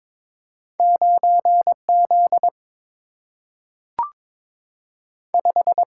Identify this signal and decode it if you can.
Morse code, heard as a keyed tone, about 22 wpm, 700 Hz. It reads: 9Z 5
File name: practice / 22wpm / Two Letter-Number Combinations - 22wpm